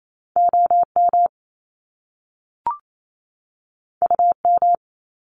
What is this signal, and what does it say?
Morse code, heard as a keyed tone, about 28 wpm, 700 Hz. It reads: OM UM
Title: Top Two Letter Combinations in Words 28wpm